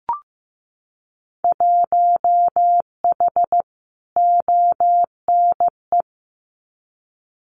Morse code, heard as a keyed tone, about 15 wpm, 700 Hz. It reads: 1H ONE
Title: Winter Field Day Exchanges 15wpm